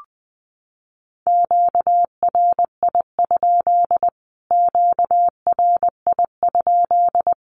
Morse code, heard as a keyed tone, about 20 wpm, 700 Hz. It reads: QRI? QRI?
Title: Q-Codes Repeated 3x 20wpm